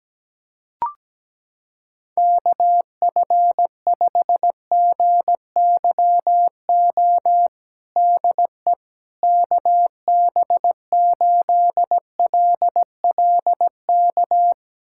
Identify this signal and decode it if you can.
Morse code, heard as a keyed tone, about 17 wpm, 700 Hz. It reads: KF5GYO DE KB8LLK
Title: USA Callsign DE Callsign 17wpm